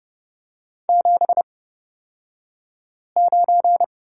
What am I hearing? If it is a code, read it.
Morse code, heard as a keyed tone, about 30 wpm, spaced as for 5 wpm, 700 Hz. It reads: 79
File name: Lesson 36d - BT - Two Characters 30wpm